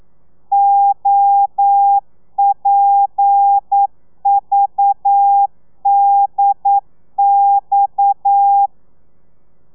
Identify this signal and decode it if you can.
Morse code, heard as a keyed tone, about 9 wpm, 800 Hz. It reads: OPVDX
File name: telegrafia / i2tfj / Lez7 40